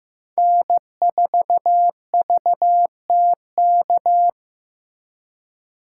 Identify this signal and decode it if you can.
Morse code, heard as a keyed tone, about 18 wpm, 700 Hz. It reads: N4VTK
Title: Speed Racing - USA 1x3 Call Signs 15wpm